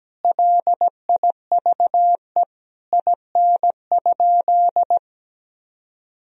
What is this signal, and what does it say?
Morse code, heard as a keyed tone, about 17 wpm, 700 Hz. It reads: LIVE IN?
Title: Sentences from Top 1000 Words - Review Part 2 - 17wpm